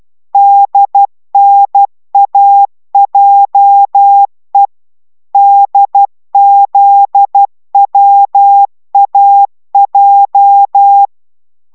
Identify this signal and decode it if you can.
Morse code, heard as a keyed tone, about 12 wpm, 800 Hz. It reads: DNAJE DZWAJ